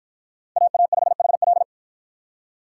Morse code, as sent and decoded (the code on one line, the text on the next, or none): .- .- ..... .... .-..
AA5HL